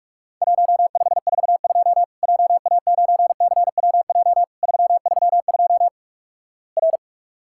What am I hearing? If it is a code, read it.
Morse code, heard as a keyed tone, about 48 wpm, 700 Hz. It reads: 1542 JA9CWJ 332 R
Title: Worked All Europe DX Contest - QTC Exchange - 45wpm